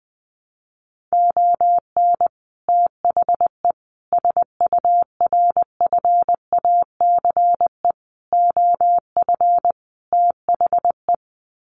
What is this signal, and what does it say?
Morse code, heard as a keyed tone, about 20 wpm, 700 Hz. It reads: ON THE SURFACE OF THE
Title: Sets of 5 Words Encore 20wpm